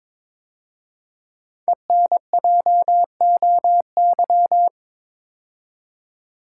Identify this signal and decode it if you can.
Morse code, heard as a keyed tone, about 22 wpm, 700 Hz. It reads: ENJOY
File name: 5 Letter Words - First 3 Letters Given - 22wpm